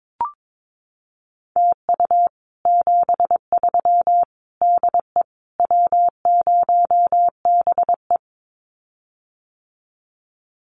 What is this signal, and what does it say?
Morse code, heard as a keyed tone, about 22 wpm, 700 Hz. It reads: TU 73 DE W0BE
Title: Common QSO Phrases Encore 22wpm